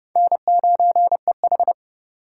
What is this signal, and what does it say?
Morse code, heard as a keyed tone, about 28 wpm, 700 Hz. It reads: N9EH